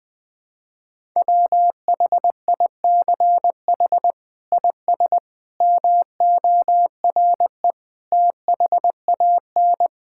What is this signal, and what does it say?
Morse code, heard as a keyed tone, about 20 wpm, 700 Hz. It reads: WHICH IS MORE THAN
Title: Sets of 8 Words - 20wpm